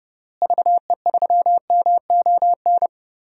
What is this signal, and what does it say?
Morse code, heard as a keyed tone, about 32 wpm, 700 Hz. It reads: VE3MON
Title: DXCC-100 Call Signs - 6 Characters - Repeated 3x - 30wpm